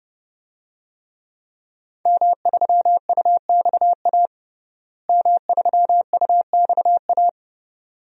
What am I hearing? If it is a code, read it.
Morse code, heard as a keyed tone, about 32 wpm, 700 Hz. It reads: M3UXA M3UXA
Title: DXCC-100 Call Signs - 5 Characters - Repeated 2x - 30wpm